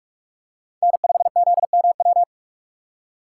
Morse code, heard as a keyed tone, about 44 wpm, 700 Hz. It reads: N5ZMW